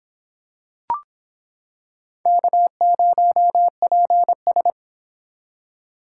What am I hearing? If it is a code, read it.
Morse code, heard as a keyed tone, about 26 wpm, 700 Hz. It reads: K0PS